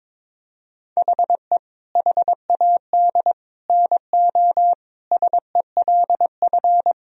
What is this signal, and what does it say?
Morse code, heard as a keyed tone, about 22 wpm, 700 Hz. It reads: HE HAD NO SELF
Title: Sentences from Top 1000 Words - Review Part 3 - 22wpm